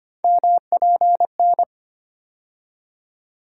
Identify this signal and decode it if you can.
Morse code, heard as a keyed tone, about 25 wpm, 700 Hz. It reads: MPN
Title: USA Call Sign Postfixes - 3 Letters 25wpm